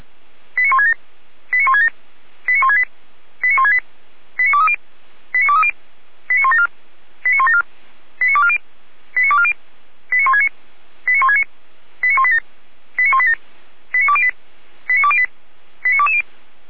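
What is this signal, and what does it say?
Five-tone selective call (ZVEI), heard as quick runs of five tones, tones of about 70 ms. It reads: 89167 89167 89168 89168 89230 89230 89174 89174 89240 89240 89169 89169 89187 89187 89208 89208 89220
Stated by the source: ZVEI five-tone selective call for address 89140, received by SDR, from the feuerwehr-orlamuende (Orlamünde fire brigade) audio server.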